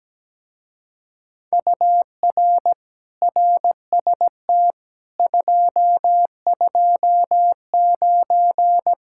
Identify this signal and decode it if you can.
Morse code, heard as a keyed tone, about 17 wpm, 700 Hz. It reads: UR RST 229
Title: Speed-Racing Common QSO Phrases - 17wpm